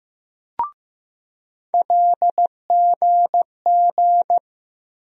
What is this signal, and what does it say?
Morse code, heard as a keyed tone, about 15 wpm, 700 Hz. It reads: LGG
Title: USA Call Sign Postfixes - 3 Letters 15wpm